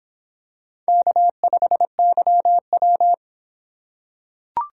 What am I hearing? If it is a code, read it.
Morse code, heard as a keyed tone, about 26 wpm, 700 Hz. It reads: K5YW